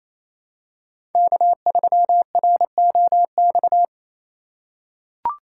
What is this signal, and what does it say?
Morse code, heard as a keyed tone, about 28 wpm, 700 Hz. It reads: K3ROX
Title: USA Call Signs - 5 Characters - 8x Character Spacing - 28wpm